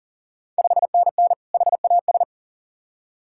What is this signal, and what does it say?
Morse code, heard as a keyed tone, about 37 wpm, 690 Hz. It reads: 5NN HAS
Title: Ontario QSO Party Exchange 40wpm